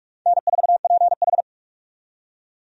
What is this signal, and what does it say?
Morse code, heard as a keyed tone, about 46 wpm, 700 Hz. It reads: N4PH
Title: USA Call Signs - 4 Characters - 45wpm